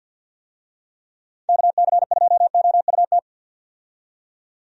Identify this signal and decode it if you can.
Morse code, heard as a keyed tone, about 46 wpm, 700 Hz. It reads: KC2OUT
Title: USA Call Signs - 6 Characters - 50wpm